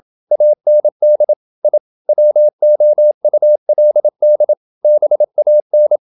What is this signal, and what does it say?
Morse code, heard as a keyed tone, about 27 wpm, 600 Hz. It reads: AND I WOULD BAN
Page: Never 26wpm | CW med Gnister